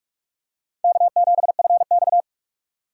Morse code, heard as a keyed tone, about 44 wpm, 700 Hz. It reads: K7FX